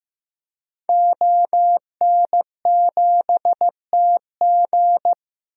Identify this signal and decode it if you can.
Morse code, heard as a keyed tone, about 15 wpm, 700 Hz. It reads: ON7TG
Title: DXCC-100 Call Signs - 5 Characters - 15wpm